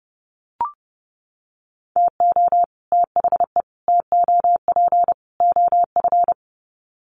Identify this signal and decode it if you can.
Morse code, heard as a keyed tone, about 30 wpm, 700 Hz. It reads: TO THE TOP OF
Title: Sets of 4 Words 30wpm